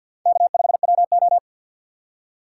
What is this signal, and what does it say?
Morse code, heard as a keyed tone, about 52 wpm, 700 Hz. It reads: K5WO
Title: USA Call Signs - 2x Character Spacing - 50wpm